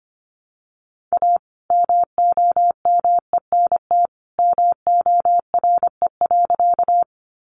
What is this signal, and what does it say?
Morse code, heard as a keyed tone, about 25 wpm, 700 Hz. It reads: A MOMENT MORE.
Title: Sentences from Top 1000 Words 25wpm